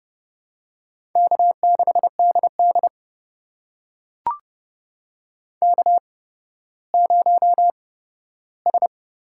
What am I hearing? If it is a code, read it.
Morse code, heard as a keyed tone, about 30 wpm, spaced as for 15 wpm, 700 Hz. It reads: K6DD K0S
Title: USA Call Signs - 4 Characters - 8x Character Spacing - 30wpm